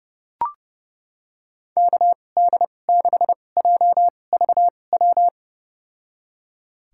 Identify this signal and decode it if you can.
Morse code, heard as a keyed tone, about 32 wpm, spaced as for 22 wpm, 700 Hz. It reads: KD6JVW
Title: USA Call Signs - 6 Characters - 2x Character Spacing - 30wpm